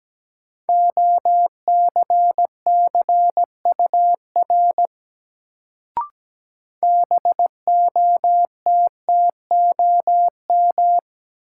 Morse code, heard as a keyed tone, about 17 wpm, 700 Hz. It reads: OCCUR BOTTOM
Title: Sentences from Top 900 Words - New Word Drill - 17wpm